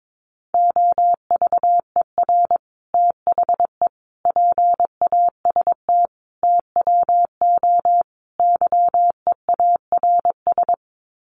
Morse code, heard as a keyed tone, about 22 wpm, 700 Hz. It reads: OVER THE PAST TWO YEARS
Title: Sets of 5 Words Encore 22wpm